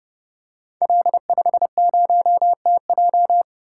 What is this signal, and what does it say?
Morse code, heard as a keyed tone, about 30 wpm, 700 Hz. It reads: L50TJ
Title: Code Groups - 5 Letter:Numbers - 30wpm